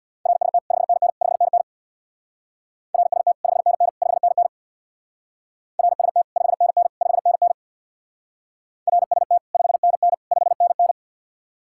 Morse code, heard as a keyed tone, about 64 wpm, 700 Hz. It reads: RST 5NN 5NN RST 5NN 5NN RST 5NN 5NN RST 5NN 5NN